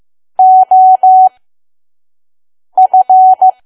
Morse code, heard as a keyed tone, about 15 wpm, spaced as for 5 wpm, 750 Hz. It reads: OF
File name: Archive / 5 WPM files / PM